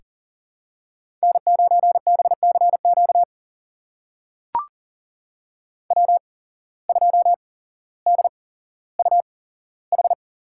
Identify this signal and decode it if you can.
Morse code, heard as a keyed tone, about 40 wpm, spaced as for 17 wpm, 700 Hz. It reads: N9BCQ W2DUH
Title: USA Call Signs - 5 Characters - 8x Character Spacing - 40wpm